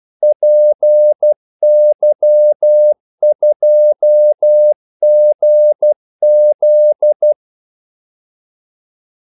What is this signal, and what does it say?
Morse code, heard as a keyed tone, about 12 wpm, 600 Hz. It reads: PY2GZ
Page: 2 ekstra ordmellemrum mellem kaldesignalerne